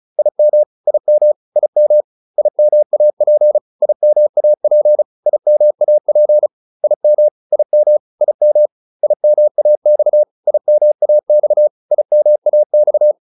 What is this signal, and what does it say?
Morse code, heard as a keyed tone, about 34 wpm, 600 Hz. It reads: IM IM IM IMAP IMAP IMAP IM IM IM IMAX IMAX IMAX